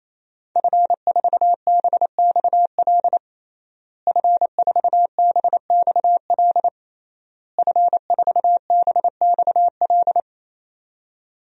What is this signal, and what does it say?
Morse code, heard as a keyed tone, about 28 wpm, 700 Hz. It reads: F4BXL F4BXL F4BXL
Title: DXCC-100 Call Signs - 5 Characters - Repeated 3x - 28wpm